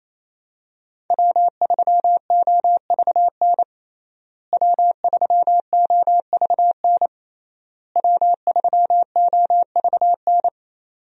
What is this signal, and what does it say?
Morse code, heard as a keyed tone, about 27 wpm, 700 Hz. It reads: W3OVN W3OVN W3OVN
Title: USA Call Signs - 5 Characters - Repeated 3x - 28wpm